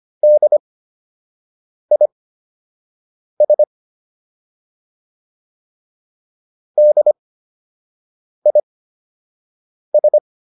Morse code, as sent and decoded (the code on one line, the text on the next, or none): -.. .. ... / -.. .. ...
DIS DIS